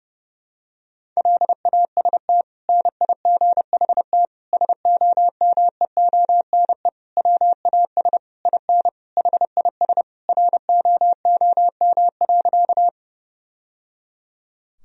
Morse code, heard as a keyed tone, about 30 wpm, 700 Hz. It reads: LAST NIGHT SOMEONE WAS IN HIS ROOM.
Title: Sentences from Top 1000 Words - Review Part 4 - 30wpm